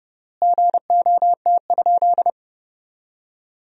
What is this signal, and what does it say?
Morse code, heard as a keyed tone, about 32 wpm, 700 Hz. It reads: GOT?
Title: 3x Word-Spacing - Sentences from Top 300 Words - 30wpm